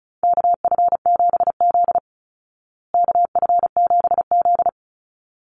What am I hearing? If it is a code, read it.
Morse code, heard as a keyed tone, about 34 wpm, 700 Hz. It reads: KF7Z KF7Z
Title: Call Signs Repeated 3x - 35wpm